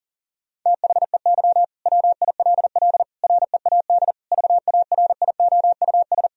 Morse code, heard as a keyed tone, about 40 wpm, 700 Hz. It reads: THEY WILL READ VARIOUS
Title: Sentences from Top 1000 Words - Review Part 4 - 40wpm